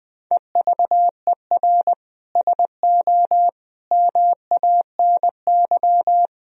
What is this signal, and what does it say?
Morse code, heard as a keyed tone, about 20 wpm, 700 Hz. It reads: EVER SO MANY
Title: Sentences from Top 1000 Words - Review Part 3 - 20wpm